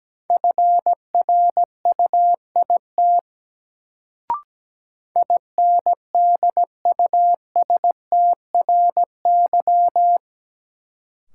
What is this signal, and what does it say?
Morse code, heard as a keyed tone, about 17 wpm, 700 Hz. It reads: FRUIT INDUSTRY
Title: Sentences from Top 1300 Words - New Word Drill Alt - 17wpm